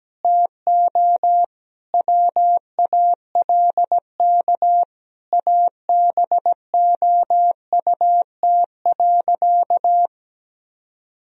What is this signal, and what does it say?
Morse code, heard as a keyed tone, about 17 wpm, 700 Hz. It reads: TO WALK ABOUT.
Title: Sentences from Top 1000 Words - Review Part 1 - 17wpm